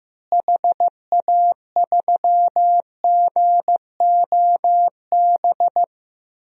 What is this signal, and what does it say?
Morse code, heard as a keyed tone, about 15 wpm, 700 Hz. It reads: HA3GOB
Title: DXCC-100 Call Signs - 6 Characters - Repeated 3x - 15wpm